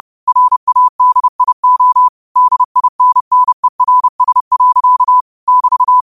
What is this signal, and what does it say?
Morse code, heard as a keyed tone, about 30 wpm, 1000 Hz. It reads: RADIO DINNERS. <BT>